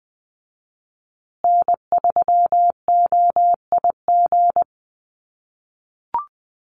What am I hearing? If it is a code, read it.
Morse code, heard as a keyed tone, about 20 wpm, 700 Hz. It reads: N3OIG
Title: USA Call Signs Repeated 2x 20wpm